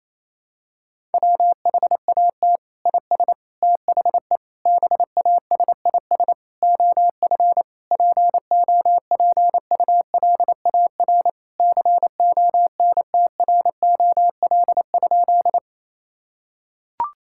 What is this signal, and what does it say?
Morse code, heard as a keyed tone, about 28 wpm, 700 Hz. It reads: WHAT IS THE BASIS OF POPULAR CONTROL?
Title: Sentences from Top 1000 Words - Review Part 5 - 28wpm